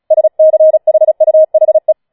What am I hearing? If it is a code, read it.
Morse code, heard as a keyed tone, about 36 wpm, 630 Hz. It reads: SCHUHE
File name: Schneewittchen Teil6 36wpm